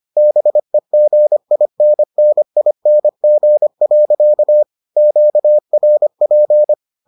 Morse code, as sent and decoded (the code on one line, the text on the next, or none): -... --. .. -. -. .. -. --. .-.-.- / --.- .-. .--.
BEGINNING. QRP